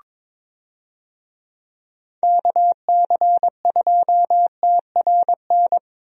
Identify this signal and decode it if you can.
Morse code, heard as a keyed tone, about 22 wpm, 700 Hz. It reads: KC2TRN
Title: USA Call Signs - 6 Characters - 22wpm